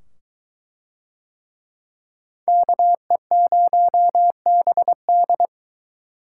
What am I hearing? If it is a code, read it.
Morse code, heard as a keyed tone, about 22 wpm, 700 Hz. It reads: KE0BD